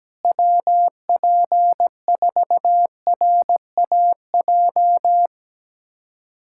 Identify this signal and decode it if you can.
Morse code, heard as a keyed tone, about 17 wpm, 700 Hz. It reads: WP4RAJ
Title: USA Callsign DE Callsign - 6 Characters - 17wpm